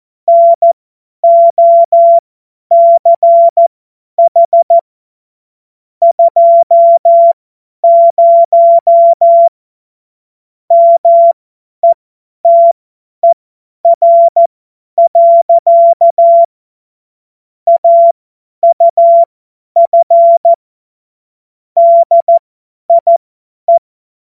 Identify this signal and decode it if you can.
Morse code, heard as a keyed tone, about 14 wpm, spaced as for 10 wpm, 680 Hz. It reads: NOCH 20 METER. AUF DIE
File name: Morsekurs / Witze / Witz 24.Oktober 2023 50 F70